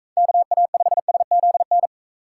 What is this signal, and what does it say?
Morse code, heard as a keyed tone, about 42 wpm, 700 Hz. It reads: KA5SZN